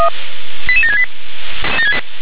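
Five-tone selective call (ZVEI), heard as quick runs of five tones, tones of about 70 ms. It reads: 88768
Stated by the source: ZVEI five-tone selective call received by SDR.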